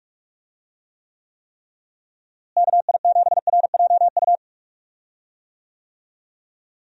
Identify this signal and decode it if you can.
Morse code, heard as a keyed tone, about 46 wpm, 700 Hz. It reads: KI7RJU
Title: USA Call Signs - 6 Characters - Repeated 3x - 45wpm